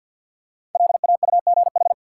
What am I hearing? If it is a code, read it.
Morse code, heard as a keyed tone, about 53 wpm, 700 Hz. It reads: LAUGH
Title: Five Letter Words - 50wpm